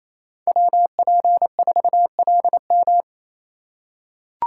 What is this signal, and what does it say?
Morse code, heard as a keyed tone, about 28 wpm, 700 Hz. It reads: WP4LM
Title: USA Call Signs - 5 Characters - 4x Character Spacing - 28wpm